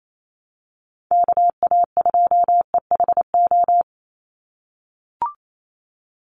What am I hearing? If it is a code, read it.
Morse code, heard as a keyed tone, about 28 wpm, 700 Hz. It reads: KA2EHO
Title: USA Call Signs 28wpm